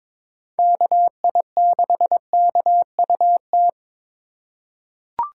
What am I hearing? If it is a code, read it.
Morse code, heard as a keyed tone, about 22 wpm, 700 Hz. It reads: KI6KUT
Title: USA Call Signs - 8x Character Spacing - 22wpm